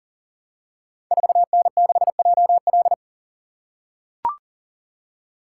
Morse code, heard as a keyed tone, about 38 wpm, 690 Hz. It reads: 4N6JL
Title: Code Groups - 5 Letter:Numbers - 40wpm